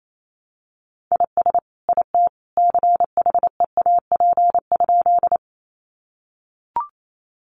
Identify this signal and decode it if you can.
Morse code, heard as a keyed tone, about 28 wpm, 700 Hz. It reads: IS IT CHEAP?
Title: Sentences from Top 1200 Words 28wpm